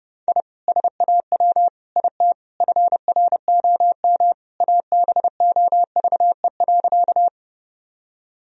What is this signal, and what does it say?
Morse code, heard as a keyed tone, about 30 wpm, 700 Hz. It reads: I SAW IT FROM ABOVE.
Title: Sentences from Top 1000 Words - Review Part 2 - 30wpm